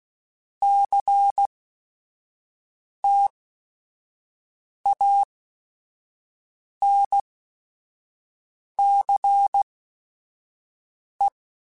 Morse code, heard as a keyed tone, about 16 wpm, spaced as for 5 wpm, 800 Hz. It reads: CTANCE
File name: Technical Subjects / Technical Subjects-mp3s / Reactance @5